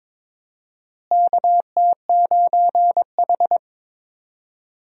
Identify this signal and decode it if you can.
Morse code, heard as a keyed tone, about 22 wpm, 700 Hz. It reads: KT9H
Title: USA Call Signs - 4 Characters - Repeated 3x - 22wpm